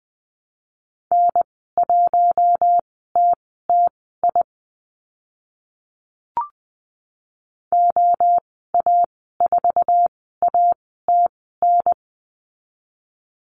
Morse code, heard as a keyed tone, about 20 wpm, spaced as for 14 wpm, 700 Hz. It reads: N1TTI OA4ATN
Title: 062 Lesson 11g - Letter h - Call Signs 20wpm